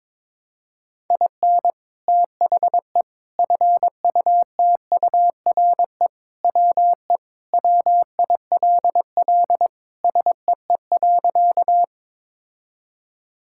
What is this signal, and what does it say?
Morse code, heard as a keyed tone, about 22 wpm, 700 Hz. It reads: IN THE FUTURE WE WILL SEE.
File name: practice / 22wpm / Sentences with 3-Word Phrases - 22wpm